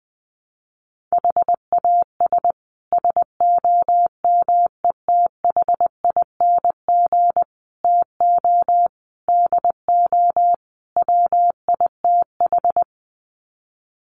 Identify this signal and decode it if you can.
Morse code, heard as a keyed tone, about 19 wpm, 700 Hz. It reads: HAS SOMETHING TO DO WITH
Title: Sets of 5 Words Encore 20wpm